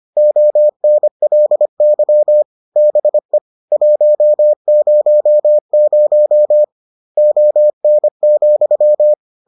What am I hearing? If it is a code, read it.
Morse code, heard as a keyed tone, about 25 wpm, 600 Hz. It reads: ONLY BE 100 ON,